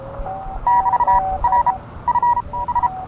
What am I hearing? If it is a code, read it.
Morse code, heard as a keyed tone, about 31 wpm, spaced as for 22 wpm, 960 Hz, with noise. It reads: <BT>R F S